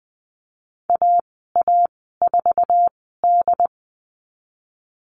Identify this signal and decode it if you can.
Morse code, heard as a keyed tone, about 20 wpm, spaced as for 15 wpm, 700 Hz. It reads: AA4D
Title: 069 Lesson 12g - Letter d - Call Signs 20wpm